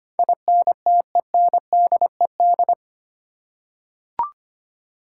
Morse code, heard as a keyed tone, about 25 wpm, 690 Hz. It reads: INTENDED
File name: 8 Letter Words - First 3 Letters Given - 25wpm